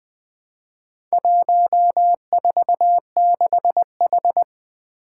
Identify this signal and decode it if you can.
Morse code, heard as a keyed tone, about 20 wpm, 700 Hz. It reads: 146H
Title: Four Challenging Letter-Number Combinations 20wpm